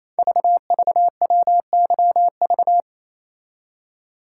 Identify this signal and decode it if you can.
Morse code, heard as a keyed tone, about 27 wpm, 700 Hz. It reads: VVWYV
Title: Code Groups - 5 Letters - 28wpm